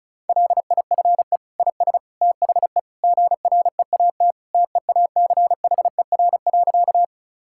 Morse code, heard as a keyed tone, about 36 wpm, 700 Hz. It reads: LIFE IS THE GREAT TEACHER.
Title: Sentences from Top 1000 Words - Review Part 6 - 35wpm